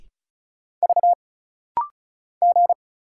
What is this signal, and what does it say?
Morse code, heard as a keyed tone, about 35 wpm, 700 Hz. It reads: V G
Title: Single Letter-Number Alt - Rapid-Fire 35wpm